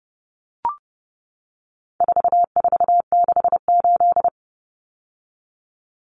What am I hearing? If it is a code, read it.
Morse code, heard as a keyed tone, about 30 wpm, 700 Hz. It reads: 4468
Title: Numbers - 4-Digit 30wpm